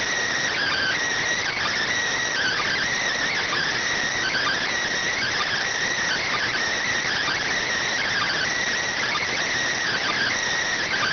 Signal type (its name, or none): Emergency Alert System (SAME) tones